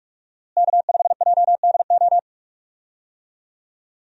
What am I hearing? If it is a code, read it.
Morse code, heard as a keyed tone, about 46 wpm, 700 Hz. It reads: K5JDO
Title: USA Call Signs - 5 Characters - Repeated 3x - 45wpm